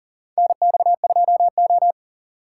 Morse code, heard as a keyed tone, about 40 wpm, 700 Hz. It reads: NX2O
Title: USA Call Signs - 2x Character Spacing - 40wpm